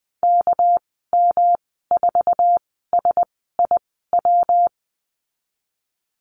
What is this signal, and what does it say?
Morse code, heard as a keyed tone, about 20 wpm, spaced as for 14 wpm, 700 Hz. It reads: KM4SIW
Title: 200 Lesson 32g - Letter k - Call Signs 20wpm